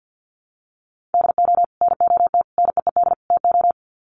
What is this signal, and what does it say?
Morse code, heard as a keyed tone, about 50 wpm, 700 Hz. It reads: DO NOT NEED TO
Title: Sets of 4 Words 50wpm